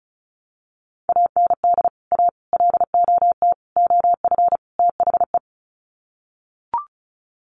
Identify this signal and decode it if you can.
Morse code, heard as a keyed tone, about 34 wpm, 700 Hz. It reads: AND A LOT OF THE
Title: Sets of 5 Words Encore 35wpm